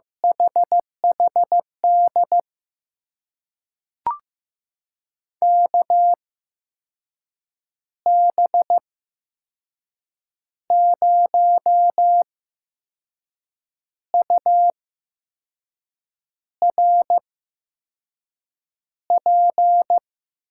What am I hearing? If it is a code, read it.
Morse code, heard as a keyed tone, about 15 wpm, spaced as for 5 wpm, 700 Hz. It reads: HHD KB0URP
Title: USA Call Signs - 6 Characters - 8x Character Spacing - 15wpm